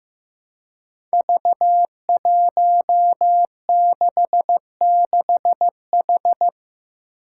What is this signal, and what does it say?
Morse code, heard as a keyed tone, about 15 wpm, 700 Hz. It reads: V166H
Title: Five Challenging Letter-Number Combinations 15wpm